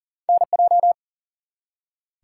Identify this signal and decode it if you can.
Morse code, heard as a keyed tone, about 39 wpm, 700 Hz. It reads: NJ
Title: US States and Canadian Provinces 40wpm